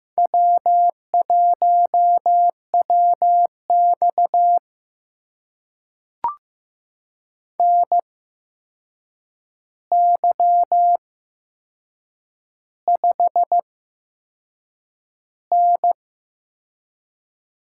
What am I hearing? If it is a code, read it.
Morse code, heard as a keyed tone, about 15 wpm, spaced as for 6 wpm, 700 Hz. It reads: W1WX NY5N
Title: USA Call Signs - 4 Characters - 8x Character Spacing - 15wpm